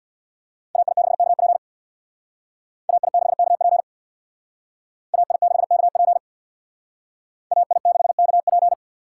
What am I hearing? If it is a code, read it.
Morse code, heard as a keyed tone, about 62 wpm, 700 Hz. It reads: AI6KP AI6KP AI6KP AI6KP
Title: Speed Racing - USA 2x2 Call Signs 50wpm